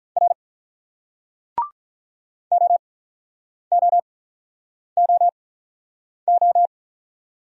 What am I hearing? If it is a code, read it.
Morse code, heard as a keyed tone, about 48 wpm, spaced as for 9 wpm, 700 Hz. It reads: R OOOO